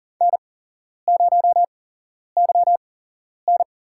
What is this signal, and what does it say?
Morse code, heard as a keyed tone, about 40 wpm, spaced as for 11 wpm, 700 Hz. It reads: N0YN
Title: USA Call Signs - 8x Character Spacing - 40wpm